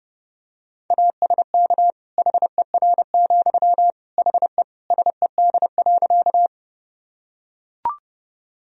Morse code, heard as a keyed tone, about 30 wpm, 700 Hz. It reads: ASK HER, HE SED.
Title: Sentences from Top 200 Words and Abbreviations 30wpm